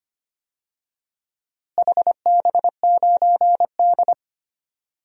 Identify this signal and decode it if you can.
Morse code, heard as a keyed tone, about 25 wpm, 700 Hz. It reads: HB9D
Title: DXCC-100 Call Signs - 4 Characters - Repeated 2x - 25wpm